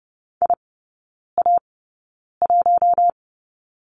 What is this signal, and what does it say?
Morse code, heard as a keyed tone, about 30 wpm, spaced as for 9 wpm, 700 Hz. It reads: IA1
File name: 081 Lesson 14e - Number 2 - Three Characters 30wpm